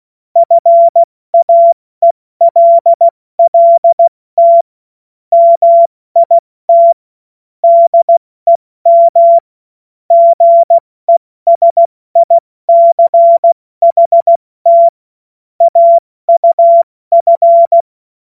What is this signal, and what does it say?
Morse code, heard as a keyed tone, about 16 wpm, 680 Hz. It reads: FAELLT MIT DEM GESICHT AUF
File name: Morsekurs / Witze / Witz 02 April 2024 70 F80